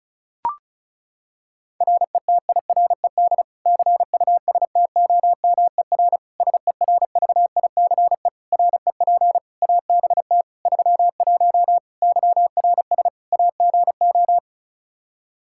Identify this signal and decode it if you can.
Morse code, heard as a keyed tone, about 36 wpm, 700 Hz. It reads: RETIRED CUSTOMER SERVICE REP ABT 31 YRS AGO
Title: Practice QSO - Exchange 3 - More Info - 35wpm